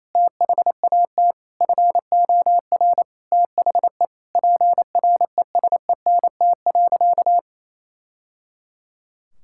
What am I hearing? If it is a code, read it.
Morse code, heard as a keyed tone, about 28 wpm, 700 Hz. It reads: THAT FOR THE PRESENT.
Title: Sentences from Top 1000 Words - Review Part 5 - 28wpm